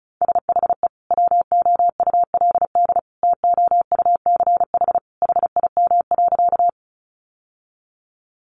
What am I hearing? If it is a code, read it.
Morse code, heard as a keyed tone, about 34 wpm, 700 Hz. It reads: SHE WOULD TOUCH HIM.
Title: Sentences from Top 1200 Words 35wpm